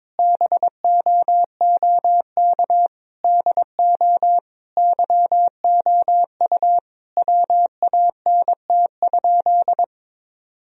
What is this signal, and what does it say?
Morse code, heard as a keyed tone, about 22 wpm, 700 Hz. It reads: BOOK DO YOU WANT?
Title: Sentences from Top 1000 Words - Review Part 4 - 22wpm